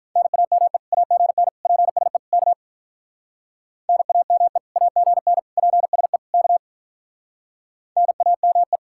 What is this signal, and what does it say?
Morse code, heard as a keyed tone, about 46 wpm, 700 Hz. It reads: NAME AGN PSE K NAME AGN PSE K NAME